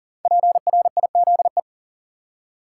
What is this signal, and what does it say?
Morse code, heard as a keyed tone, about 40 wpm, 700 Hz. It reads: PRIZE
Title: Five Letter Words - 40wpm